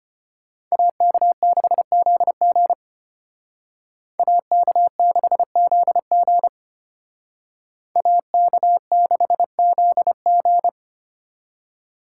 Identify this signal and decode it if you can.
Morse code, heard as a keyed tone, about 32 wpm, 700 Hz. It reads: AK6ZG AK6ZG AK6ZG